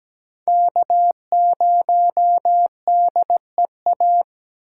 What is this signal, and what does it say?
Morse code, heard as a keyed tone, about 17 wpm, 700 Hz. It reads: K 0 D E A
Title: USA Call Signs - 5 Characters - 8x Character Spacing - 17wpm